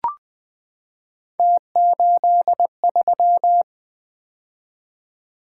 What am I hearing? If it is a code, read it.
Morse code, heard as a keyed tone, about 20 wpm, 700 Hz. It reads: T83
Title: Three Letter-Number Combinations - 20wpm